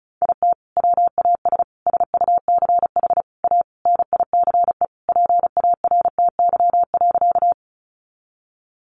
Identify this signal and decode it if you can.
Morse code, heard as a keyed tone, about 34 wpm, 700 Hz. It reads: IT WAS SUCH A NICE PARTY.
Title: Sentences from Top 1400 Words 35wpm